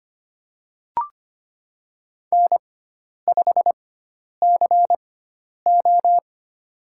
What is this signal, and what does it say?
Morse code, heard as a keyed tone, about 25 wpm, spaced as for 10 wpm, 700 Hz. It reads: N5CO
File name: Lesson 16g - Letter c - Call Signs 25wpm